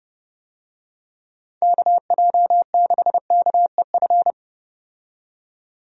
Morse code, heard as a keyed tone, about 30 wpm, 700 Hz. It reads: KJ6KEF
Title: USA Call Signs - 6 Characters - 30wpm